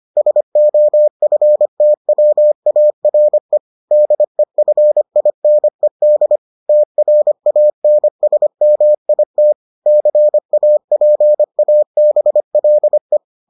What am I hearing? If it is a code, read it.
Morse code, heard as a keyed tone, about 25 wpm, 600 Hz. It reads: SOFTWARE DEFINED TRANSMIT CAPABLE